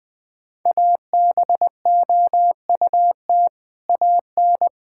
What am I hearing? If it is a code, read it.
Morse code, heard as a keyed tone, about 19 wpm, 700 Hz. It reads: ABOUT AN
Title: Sentences from Top 1000 Words - Review Part 6 - 20wpm